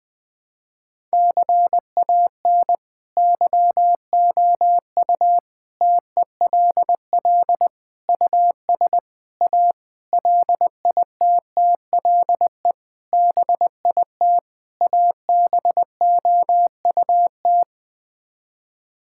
Morse code, heard as a keyed tone, about 20 wpm, 700 Hz. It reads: CAN YOU TELL US A LITTLE BIT ABOUT
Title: Sets of 8 Words - 20wpm